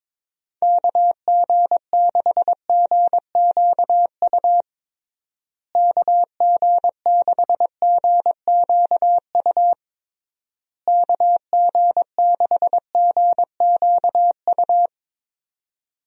Morse code, heard as a keyed tone, about 22 wpm, 700 Hz. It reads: KG6GQU KG6GQU KG6GQU
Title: USA Call Signs - 6 Characters - Repeated 3x - 22wpm